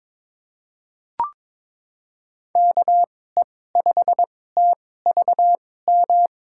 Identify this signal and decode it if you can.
Morse code, heard as a keyed tone, about 22 wpm, spaced as for 16 wpm, 700 Hz. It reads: KE5TVM
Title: USA Call Signs - 6 Characters - 2x Character Spacing - 22wpm